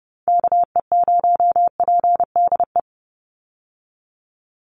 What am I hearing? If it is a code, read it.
Morse code, heard as a keyed tone, about 30 wpm, 700 Hz. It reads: KE0PDE
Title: USA Call Signs 30wpm